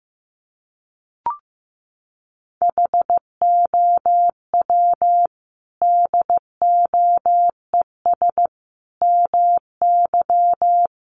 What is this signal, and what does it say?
Morse code, heard as a keyed tone, about 15 wpm, 700 Hz. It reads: HOW DOES MY
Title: Sentences from Top 1500 Words 15wpm